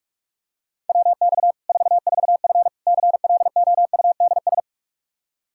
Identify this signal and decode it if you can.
Morse code, heard as a keyed tone, about 41 wpm, 700 Hz. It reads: WX 44F CLOUDS
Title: K1USN SST QSO Practice 45wpm